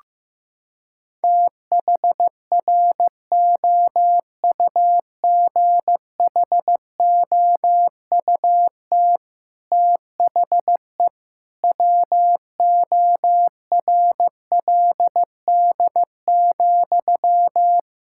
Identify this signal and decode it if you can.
Morse code, heard as a keyed tone, about 15 wpm, 700 Hz. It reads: THROUGHOUT THE WORLD,
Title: Sentences from Top 1000 Words - Review Part 5 - 15wpm